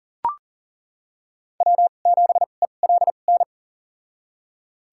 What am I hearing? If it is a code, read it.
Morse code, heard as a keyed tone, about 40 wpm, spaced as for 29 wpm, 700 Hz. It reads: W7ELN
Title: USA Call Signs - 5 Characters - 2x Character Spacing - 40wpm